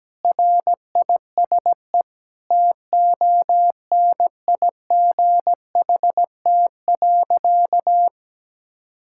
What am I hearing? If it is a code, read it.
Morse code, heard as a keyed tone, about 17 wpm, 700 Hz. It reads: RISE TONIGHT.
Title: Sentences from Top 1000 Words - Review Part 6 - 17wpm